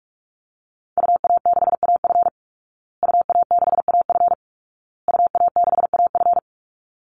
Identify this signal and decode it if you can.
Morse code, heard as a keyed tone, about 44 wpm, 700 Hz. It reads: UA6AF UA6AF UA6AF
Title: Call Signs Repeated 2x - 45wpm